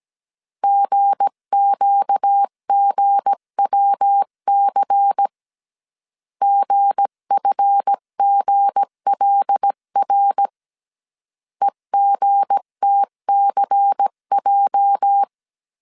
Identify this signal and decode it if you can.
Morse code, heard as a keyed tone, about 17 wpm, 800 Hz. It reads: GQGWC GFGLR EGTCJ
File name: CW / DET / 1 9 1500C S B 16wpm 2856